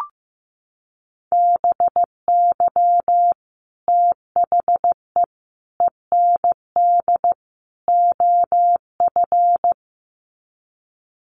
Morse code, heard as a keyed tone, about 14 wpm, 700 Hz. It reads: BY THE END OF
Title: Sets of 4 Words 15wpm